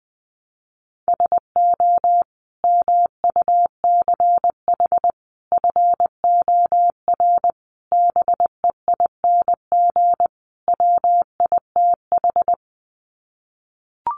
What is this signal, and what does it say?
Morse code, heard as a keyed tone, about 19 wpm, 700 Hz. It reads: SO MUCH FOR BEING WITH
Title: Sets of 5 Words Encore 20wpm